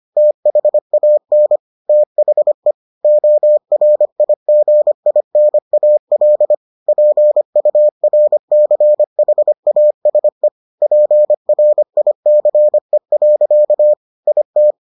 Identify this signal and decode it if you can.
Morse code, heard as a keyed tone, about 25 wpm, 600 Hz. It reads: THAN THE ORIGINAL PURCHASE PRICE. IT